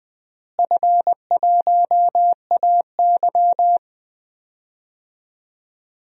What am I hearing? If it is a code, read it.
Morse code, heard as a keyed tone, about 20 wpm, 700 Hz. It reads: F1AY
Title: DXCC-100 Call Signs - 4 Characters - Repeated 3x - 20wpm